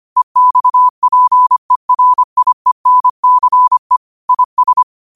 Morse code, heard as a keyed tone, about 25 wpm, 1000 Hz. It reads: EXPERIENCE IS
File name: qotdfiles / 25wpm